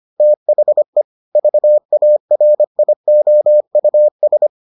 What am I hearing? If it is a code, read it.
Morse code, heard as a keyed tone, about 25 wpm, 600 Hz. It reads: THE VARIOUS